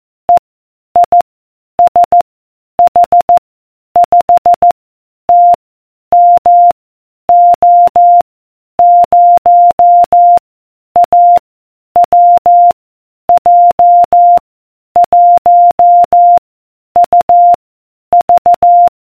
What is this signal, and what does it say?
Morse code, heard as a keyed tone, about 14 wpm, 700 Hz. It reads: E I S H 5 T M O 0 A W J 1 U V